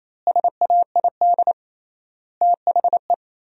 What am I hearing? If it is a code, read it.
Morse code, heard as a keyed tone, about 28 wpm, 700 Hz. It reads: SAID THE
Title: 3x Word-Spacing - Sentences from Top 200 Words - 28wpm